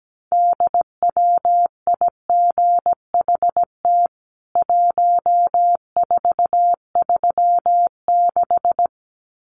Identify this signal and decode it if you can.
Morse code, heard as a keyed tone, about 17 wpm, 700 Hz. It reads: DWIGHT 1436
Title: CWT Exchange 17wpm